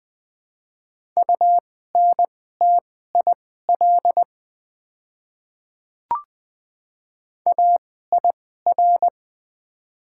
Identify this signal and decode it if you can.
Morse code, heard as a keyed tone, about 20 wpm, spaced as for 14 wpm, 700 Hz. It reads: UNTIL AIR
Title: 2x Character Spacing - Top 300 Words - 20wpm